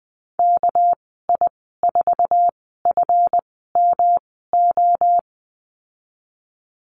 Morse code, heard as a keyed tone, about 20 wpm, spaced as for 15 wpm, 700 Hz. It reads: KI4FMO
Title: 219 Lesson 35g - Number 0 - Call Signs 20wpm